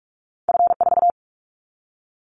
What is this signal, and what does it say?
Morse code, heard as a keyed tone, about 44 wpm, 700 Hz. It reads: F4
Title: Callsign Prefixes for Top 25 DXCC - 45wpm